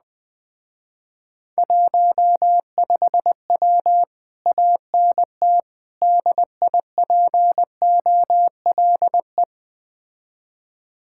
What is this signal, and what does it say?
Morse code, heard as a keyed tone, about 20 wpm, 700 Hz. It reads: 15W ANT DIPOLE
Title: Speed-Racing Common QSO Phrases Encore - 17wpm